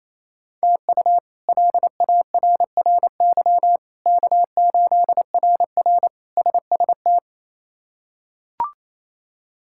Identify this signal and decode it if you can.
Morse code, heard as a keyed tone, about 27 wpm, 700 Hz. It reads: TU LARRY K8RR SST
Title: K1USN SST QSO Practice - 28wpm